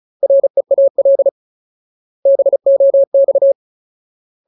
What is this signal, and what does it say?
Morse code, heard as a keyed tone, about 34 wpm, 550 Hz. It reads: REAL BOX
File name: Ultimate Word List / 35WPM Ultimate